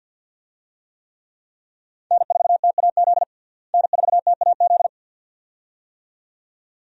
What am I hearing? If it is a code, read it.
Morse code, heard as a keyed tone, about 52 wpm, 700 Hz. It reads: N4TAZ N4TAZ
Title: USA Call Signs - 5 Characters - Repeated 2x - 50wpm